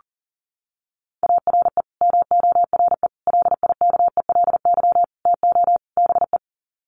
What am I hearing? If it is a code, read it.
Morse code, heard as a keyed tone, about 40 wpm, 700 Hz. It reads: ARE MORE LIKELY TO BE
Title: Sets of 5 Words 40wpm